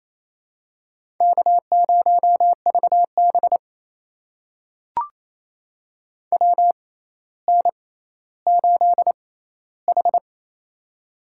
Morse code, heard as a keyed tone, about 28 wpm, spaced as for 14 wpm, 700 Hz. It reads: K0VB WN8H
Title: USA Call Signs - 4 Characters - 6x Character Spacing - 28wpm